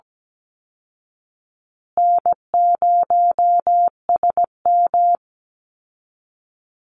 Morse code, heard as a keyed tone, about 17 wpm, 700 Hz. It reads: N0SM
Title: Call Signs Repeated 2x - 17wpm